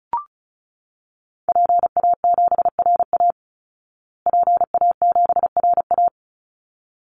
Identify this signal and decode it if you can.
Morse code, heard as a keyed tone, about 34 wpm, 700 Hz. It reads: PA7RA PA7RA
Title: Call Signs Repeated 3x - 35wpm